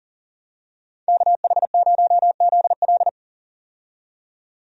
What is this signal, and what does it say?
Morse code, heard as a keyed tone, about 37 wpm, 700 Hz. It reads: KH0ZL
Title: USA Call Signs - 5 Characters - 40wpm